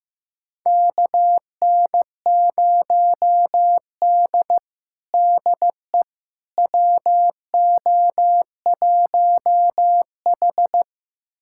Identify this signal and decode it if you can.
Morse code, heard as a keyed tone, about 16 wpm, 700 Hz. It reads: KN0D DE WO1H
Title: USA Callsign DE Callsign - 4 Characters - 15wpm